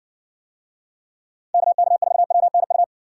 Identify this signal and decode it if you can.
Morse code, heard as a keyed tone, about 60 wpm, 700 Hz. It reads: KK4WNU